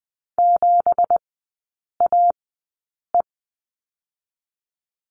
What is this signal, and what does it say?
Morse code, heard as a keyed tone, about 20 wpm, spaced as for 8 wpm, 700 Hz. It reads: 7AE
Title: 184 Lesson 30e - Letter b - Three Characters 20wpm